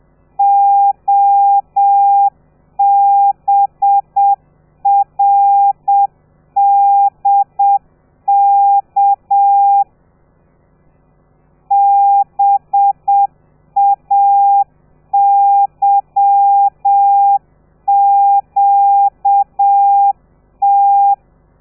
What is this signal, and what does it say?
Morse code, heard as a keyed tone, about 7 wpm, 800 Hz. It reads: OBRDK BAYQT
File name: telegrafia / i2tfj / Lez13 30